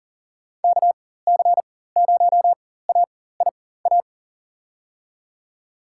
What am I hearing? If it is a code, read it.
Morse code, heard as a keyed tone, about 40 wpm, spaced as for 19 wpm, 700 Hz. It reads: KC0AIA
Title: USA Call Signs - 6 Characters - 4x Character Spacing - 40wpm